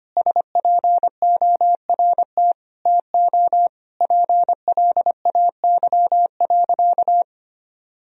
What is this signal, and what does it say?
Morse code, heard as a keyed tone, about 25 wpm, 700 Hz. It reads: SPORT TO PLAY.
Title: Sentences from Top 1000 Words - Review Part 4 - 25wpm